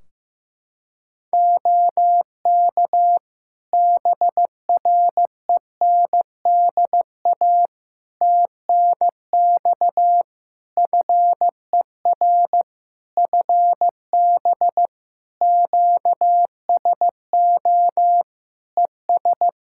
Morse code, heard as a keyed tone, about 16 wpm, 700 Hz. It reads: OK BRENDA TNX FER FB QSO ES
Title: Practice QSO - Exchange 4 - Ending - 15wpm